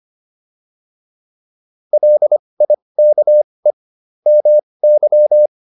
Morse code, heard as a keyed tone, about 25 wpm, spaced as for 20 wpm, 600 Hz. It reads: LIKE MY
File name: PR301 20